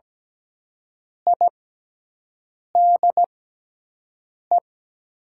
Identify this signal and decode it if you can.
Morse code, heard as a keyed tone, about 17 wpm, spaced as for 6 wpm, 700 Hz. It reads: IDE
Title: 6x Character Spacing - Top 100 Words - 17wpm